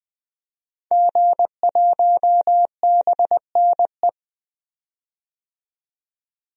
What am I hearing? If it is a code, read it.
Morse code, heard as a keyed tone, about 21 wpm, 700 Hz. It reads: G1BNE
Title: DXCC-100 Call Signs - 5 Characters - 20wpm